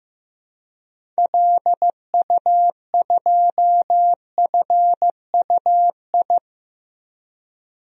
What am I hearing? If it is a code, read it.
Morse code, heard as a keyed tone, about 16 wpm, 700 Hz. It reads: LU2FUI
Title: DXCC-100 Call Signs - 6 Characters - 15wpm